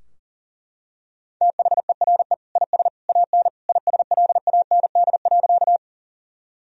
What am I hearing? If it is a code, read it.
Morse code, heard as a keyed tone, about 40 wpm, 700 Hz. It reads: THERE IS AN ISLAND.
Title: Sentences from Top 1000 Words - Review Part 3 - 40wpm